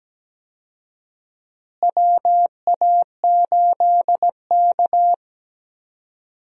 Morse code, heard as a keyed tone, about 17 wpm, 700 Hz. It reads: WA8K
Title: USA Call Signs - 4 Characters - Repeated 2x - 17wpm